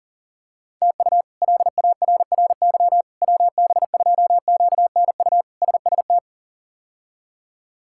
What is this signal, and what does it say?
Morse code, heard as a keyed tone, about 40 wpm, 700 Hz. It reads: TU LARRY WB2QNU SST
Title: K1USN SST QSO Practice - 40wpm